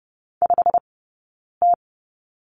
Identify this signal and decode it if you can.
Morse code, heard as a keyed tone, about 30 wpm, 700 Hz. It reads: 5 T
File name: Lesson 20d - Number 3 - Two Characters 30wpm